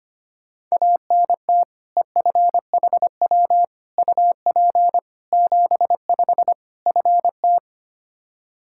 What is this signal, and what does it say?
Morse code, heard as a keyed tone, about 25 wpm, 700 Hz. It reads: ANT EFHW UP 75 FT
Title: Practice QSO - Exchange 2 - Info - 25wpm